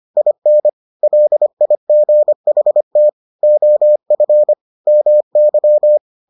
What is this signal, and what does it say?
Morse code, heard as a keyed tone, about 25 wpm, 600 Hz. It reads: IN LIGHT OF MY